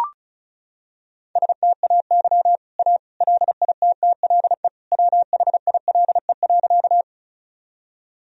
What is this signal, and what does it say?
Morse code, heard as a keyed tone, about 36 wpm, 700 Hz. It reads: STAY A LITTLE WHILE.
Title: Sentences from Top 1000 Words - Review Part 1 - 35wpm